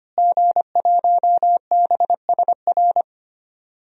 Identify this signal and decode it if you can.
Morse code, heard as a keyed tone, about 25 wpm, 700 Hz. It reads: G1BSR
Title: DXCC-100 Call Signs - 5 Characters - Repeated 3x - 25wpm